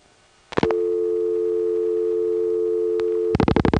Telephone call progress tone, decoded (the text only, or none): dial tone